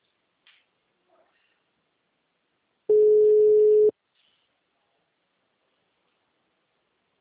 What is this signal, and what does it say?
Ringback tone (Europe, 425 Hz).